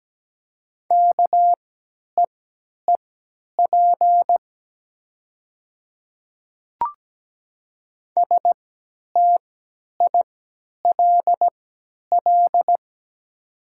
Morse code, heard as a keyed tone, about 17 wpm, spaced as for 10 wpm, 700 Hz. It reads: KEEP STILL
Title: 3x Character Spacing - Top 300 Words - 17wpm